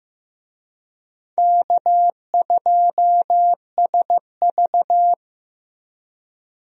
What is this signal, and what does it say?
Morse code, heard as a keyed tone, about 15 wpm, 700 Hz. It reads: K2SV
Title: USA Call Signs - 4 Characters - Repeated 3x - 15wpm